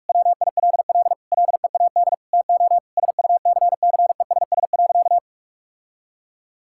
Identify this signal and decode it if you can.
Morse code, heard as a keyed tone, about 46 wpm, 700 Hz. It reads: WILL LEAD TO SUCCESS.
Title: Sentences from Top 1000 Words - Review Part 6 - 45wpm